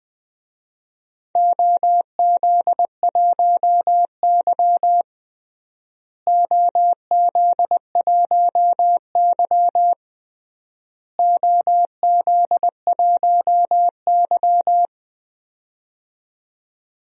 Morse code, heard as a keyed tone, about 20 wpm, 700 Hz. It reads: OZ1Y OZ1Y OZ1Y
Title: DXCC-100 Call Signs - 4 Characters - Repeated 3x - 20wpm